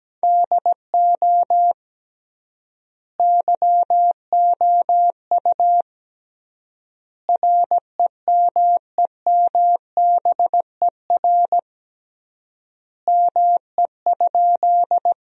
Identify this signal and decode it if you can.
Morse code, heard as a keyed tone, about 17 wpm, 700 Hz. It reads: DO YOU REMEMBER ME?
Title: 3x Word-Spacing - Sentences from Top 400 Words - 17wpm